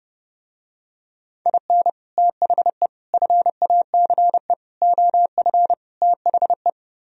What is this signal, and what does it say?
Morse code, heard as a keyed tone, about 29 wpm, 700 Hz. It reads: IN THE FACE OF THE